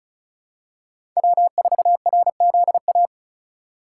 Morse code, heard as a keyed tone, about 36 wpm, 700 Hz. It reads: W4RZA
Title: USA Call Signs - 3x Character Spacing - 35wpm